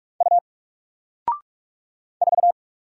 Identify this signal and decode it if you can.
Morse code, heard as a keyed tone, about 44 wpm, 700 Hz. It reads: U 4
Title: Single Character (Letter:Number:Punctuation:Prosign) 45wpm